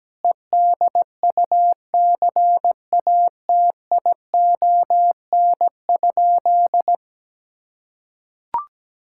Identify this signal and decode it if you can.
Morse code, heard as a keyed tone, about 17 wpm, 700 Hz. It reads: EDUCATION?
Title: Sentences from Top 1600 Words 17wpm